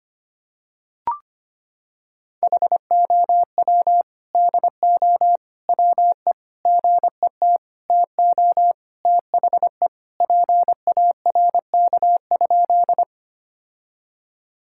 Morse code, heard as a keyed tone, about 25 wpm, 700 Hz. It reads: HOW DO WE GET TO THE PARK?
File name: practice / 25wpm / Sentences with 4-Word Phrases - 25wpm